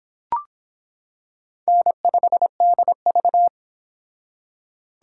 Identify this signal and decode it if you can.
Morse code, heard as a keyed tone, about 25 wpm, 700 Hz. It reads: N5DV